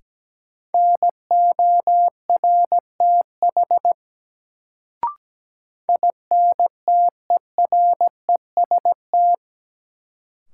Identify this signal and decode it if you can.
Morse code, heard as a keyed tone, about 17 wpm, 700 Hz. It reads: NORTH INTEREST
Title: Sentences from Top 400 Words - New Word Drill - 17wpm